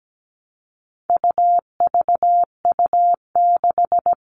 Morse code, heard as a keyed tone, about 17 wpm, 700 Hz. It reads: UVU6
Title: All Cut Numbers - 4-Digit 17wpm